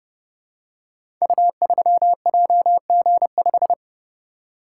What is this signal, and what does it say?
Morse code, heard as a keyed tone, about 31 wpm, 700 Hz. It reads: U3JG5
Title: Code Groups - 5 Letter:Numbers - 30wpm